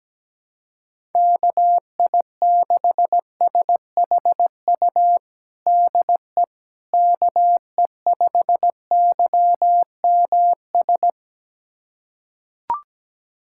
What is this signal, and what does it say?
Morse code, heard as a keyed tone, about 17 wpm, 700 Hz. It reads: KI6SHU DE KE5YMS
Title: USA Callsign DE Callsign - 6 Characters - 17wpm